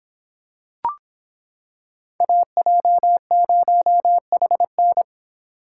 Morse code, heard as a keyed tone, about 25 wpm, 700 Hz. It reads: AJ0HN